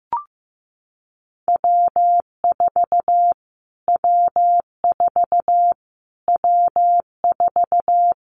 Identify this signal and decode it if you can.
Morse code, heard as a keyed tone, about 15 wpm, 700 Hz. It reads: W4 W4 W4
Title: Callsign Prefixes for Top 25 DXCC - Repeated 3x - 15wpm